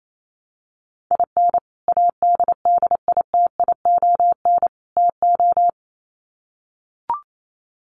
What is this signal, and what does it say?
Morse code, heard as a keyed tone, about 28 wpm, 700 Hz. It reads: IN ADDITION TO
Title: Sets of 3 Words 28wpm